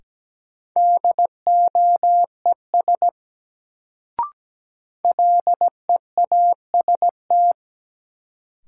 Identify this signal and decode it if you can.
Morse code, heard as a keyed tone, about 17 wpm, 700 Hz. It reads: DOES LEAST
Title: Sets of 2 Words - New Word Drill - 17wpm